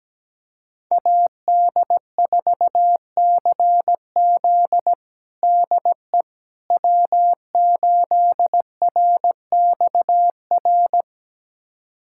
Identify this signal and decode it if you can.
Morse code, heard as a keyed tone, about 17 wpm, 700 Hz. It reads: AD4CZ DE W8RXR
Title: USA Callsign DE Callsign - 5 Characters - 17wpm